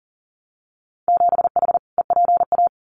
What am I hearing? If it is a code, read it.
Morse code, heard as a keyed tone, about 41 wpm, 700 Hz. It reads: 7H EPA
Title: Winter Field Day Exchanges Encore 40wpm